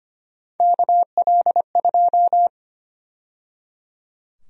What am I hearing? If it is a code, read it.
Morse code, heard as a keyed tone, about 25 wpm, 700 Hz. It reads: KL2
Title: USA Call Sign Prefixes - 25wpm